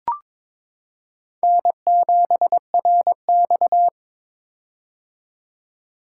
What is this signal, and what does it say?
Morse code, heard as a keyed tone, about 22 wpm, 700 Hz. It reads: N7RX
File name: practice / 22wpm / Four Letter-Number Combinations - 22wpm